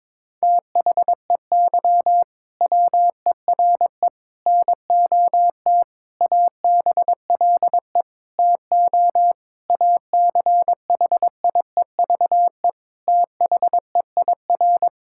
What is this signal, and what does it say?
Morse code, heard as a keyed tone, about 22 wpm, 700 Hz. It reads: THEY WERE NOT ABLE TO ACHIEVE THEIR
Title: Sentences from Top 1000 Words - Review Part 6 - 22wpm